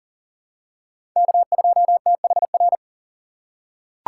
Morse code, heard as a keyed tone, about 37 wpm, 700 Hz. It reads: K2THR
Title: USA Call Signs - 5 Characters - 4x Character Spacing - 40wpm